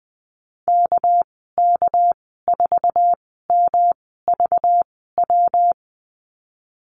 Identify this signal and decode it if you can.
Morse code, heard as a keyed tone, about 19 wpm, spaced as for 14 wpm, 700 Hz. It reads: KK4MVW
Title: 200 Lesson 32g - Letter k - Call Signs 20wpm